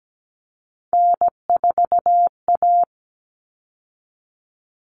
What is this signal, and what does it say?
Morse code, heard as a keyed tone, about 17 wpm, 700 Hz. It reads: N4A
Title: All Cut Numbers - 3-Digit 17wpm